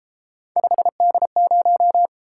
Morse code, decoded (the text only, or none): 5D0